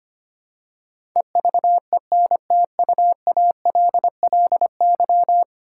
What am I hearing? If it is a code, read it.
Morse code, heard as a keyed tone, about 25 wpm, 700 Hz. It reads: EVENTUALLY
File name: Ten Letter Words - 25wpm